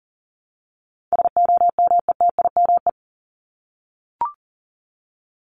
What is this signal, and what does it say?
Morse code, heard as a keyed tone, about 40 wpm, 700 Hz. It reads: SOMETIME
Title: ES:S Suffix - Top 100 Words with Base Words 40wpm